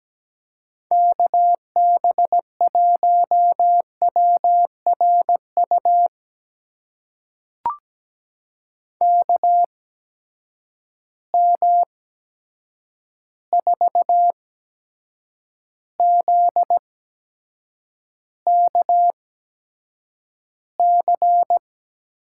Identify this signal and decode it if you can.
Morse code, heard as a keyed tone, about 17 wpm, spaced as for 7 wpm, 700 Hz. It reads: KB1WRU KM4ZKC
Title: USA Call Signs - 6 Characters - 8x Character Spacing - 17wpm